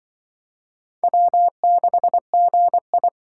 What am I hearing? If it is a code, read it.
Morse code, heard as a keyed tone, about 25 wpm, 700 Hz. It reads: W6GI
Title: Speed Racing - USA 1x2 Call Signs 20wpm